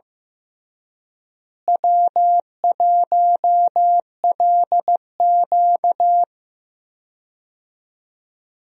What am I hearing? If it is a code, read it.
Morse code, heard as a keyed tone, about 15 wpm, 700 Hz. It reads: W1LQ
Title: USA Call Signs - 4 Characters - Repeated 3x - 15wpm